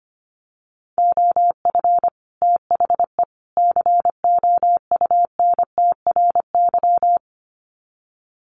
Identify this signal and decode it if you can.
Morse code, heard as a keyed tone, about 25 wpm, 700 Hz. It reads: OF THE COUNTRY
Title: Sets of 3 Words 25wpm